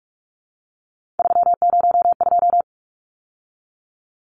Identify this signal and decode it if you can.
Morse code, heard as a keyed tone, about 44 wpm, 700 Hz. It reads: 302
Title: Numbers - 3-Digit 45wpm